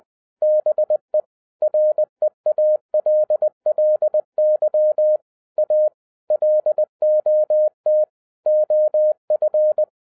Morse code, heard as a keyed tone, about 20 wpm, 610 Hz. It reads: BE REALLY A LOT OF
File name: 2012-01-mp3 / n16-THROUGH EIGHT MONTHS-20wpm